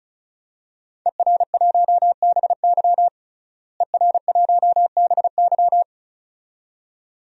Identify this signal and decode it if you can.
Morse code, heard as a keyed tone, about 34 wpm, 700 Hz. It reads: ER1BY ER1BY
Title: DXCC-100 Call Signs - 5 Characters - Repeated 2x - 35wpm